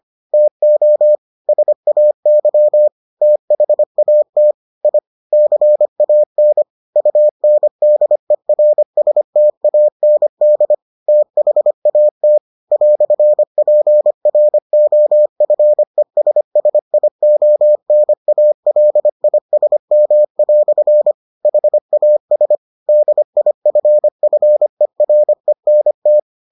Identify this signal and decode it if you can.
Morse code, heard as a keyed tone, about 25 wpm, 600 Hz. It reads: TO SAY THAT I CAN UNDERSTAND THAT "PROFESSIONALISM" HAS DIFFERENT